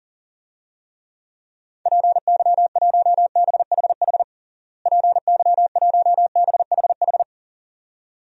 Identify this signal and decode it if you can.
Morse code, heard as a keyed tone, about 40 wpm, 700 Hz. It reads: PY1BHH PY1BHH
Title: DXCC-100 Call Signs - 6 Characters - Repeated 2x - 40wpm